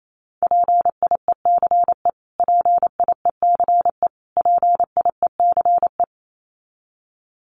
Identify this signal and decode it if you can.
Morse code, heard as a keyed tone, about 28 wpm, 700 Hz. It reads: PIECE PIECE PIECE
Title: Top 500 Words Repeated 28wpm